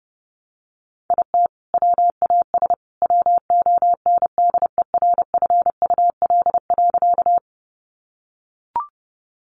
Morse code, heard as a keyed tone, about 30 wpm, 700 Hz. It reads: IT WAS WONDERFUL.
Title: Sentences from Top 1100 Words 30wpm